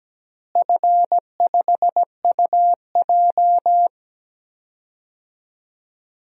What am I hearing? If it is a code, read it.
Morse code, heard as a keyed tone, about 17 wpm, 700 Hz. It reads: F5UJ
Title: DXCC-100 Call Signs - 4 Characters - Repeated 2x - 17wpm